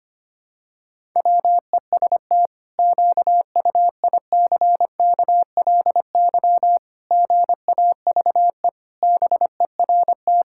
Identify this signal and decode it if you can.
Morse code, heard as a keyed tone, about 25 wpm, 700 Hz. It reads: WEST QUICKLY GAVE BERT
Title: Pangram Challenge v2 - 25wpm